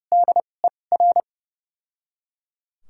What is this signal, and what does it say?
Morse code, heard as a keyed tone, about 30 wpm, spaced as for 22 wpm, 700 Hz. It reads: DER
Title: 2x Character Spacing - Top 300 Words - 30wpm